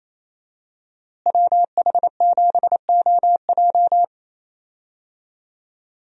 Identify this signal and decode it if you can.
Morse code, heard as a keyed tone, about 28 wpm, 700 Hz. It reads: WH7OJ
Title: USA Call Signs - 5 Characters - Repeated 3x - 28wpm